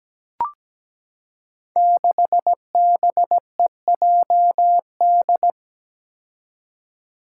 Morse code, heard as a keyed tone, about 17 wpm, 700 Hz. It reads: 6BEJD
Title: Five Challenging Letter-Number Combinations 17wpm